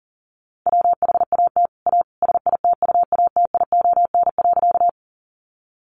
Morse code, heard as a keyed tone, about 40 wpm, 700 Hz. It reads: WHAT A SITUATION.
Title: Sentences from Top 1200 Words 40wpm